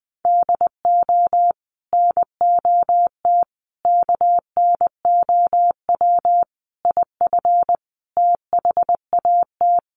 Morse code, heard as a keyed tone, about 20 wpm, 700 Hz. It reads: DO NOT KNOW IF THAT
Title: Sets of 5 Words Encore2 20wpm